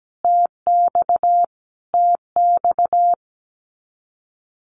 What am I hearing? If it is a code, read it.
Morse code, heard as a keyed tone, about 17 wpm, 700 Hz. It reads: TX TX
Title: Common QSO Phrases Encore 17wpm